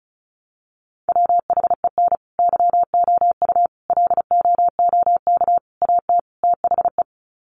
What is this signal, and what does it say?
Morse code, heard as a keyed tone, about 34 wpm, 700 Hz. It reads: WHEN YOU LOOK AT THE
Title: Sets of 5 Words 35wpm